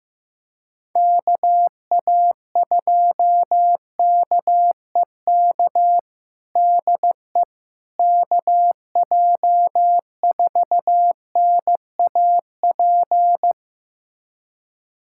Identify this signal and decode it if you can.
Morse code, heard as a keyed tone, about 15 wpm, 700 Hz. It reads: KA2KEK DE KJ4NAP
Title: USA Callsign DE Callsign - 6 Characters - 15wpm